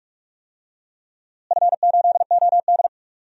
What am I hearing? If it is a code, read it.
Morse code, heard as a keyed tone, about 46 wpm, 700 Hz. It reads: F8OD
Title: DXCC-100 Call Signs - 4 Characters - Repeated 2x - 45wpm